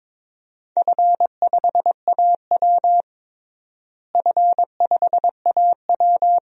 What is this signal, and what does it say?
Morse code, heard as a keyed tone, about 22 wpm, 700 Hz. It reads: F5AW F5AW
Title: DXCC-100 Call Signs - 4 Characters - Repeated 2x - 22wpm